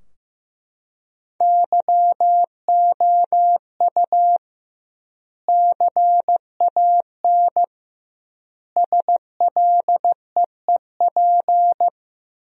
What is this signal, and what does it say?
Morse code, heard as a keyed tone, about 15 wpm, 700 Hz. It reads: YOU CAN SLEEP
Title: 2x Word-Spacing - Sentences from Top 500 Words - 15wpm